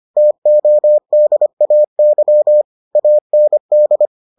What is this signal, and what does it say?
Morse code, heard as a keyed tone, about 25 wpm, 600 Hz. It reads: TODAY AND